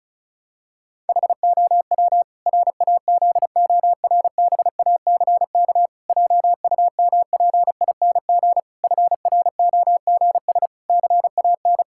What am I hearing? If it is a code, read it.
Morse code, heard as a keyed tone, about 36 wpm, 700 Hz. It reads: HOW RAZORBACK JUMPING FROGS CAN
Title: Pangram Challenge v2 - 35wpm